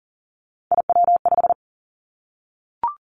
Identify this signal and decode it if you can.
Morse code, heard as a keyed tone, about 41 wpm, 700 Hz. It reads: IW5
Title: Three Challenging Letter-Number Combinations 40wpm